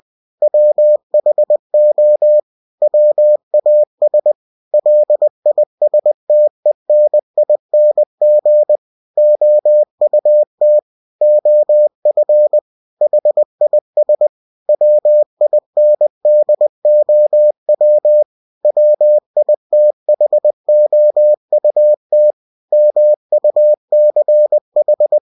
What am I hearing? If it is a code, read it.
Morse code, heard as a keyed tone, about 20 wpm, 600 Hz. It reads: WHO WAS LISTENING OUT OF HIS WINDOW WITHOUT MUCH
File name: cw-books / winnie-the-pooh / winnie-20wpm-page-0101